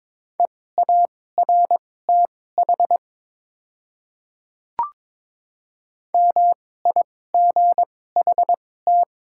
Morse code, heard as a keyed tone, about 22 wpm, spaced as for 16 wpm, 700 Hz. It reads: EARTH MIGHT
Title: 2x Character Spacing - Top 300 Words - 22wpm